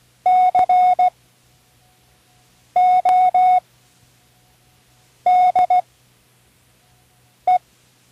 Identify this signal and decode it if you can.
Morse code, heard as a keyed tone, about 16 wpm, spaced as for 5 wpm, 730 Hz. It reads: CODE
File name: Learning Morse Code (from Tune in the World)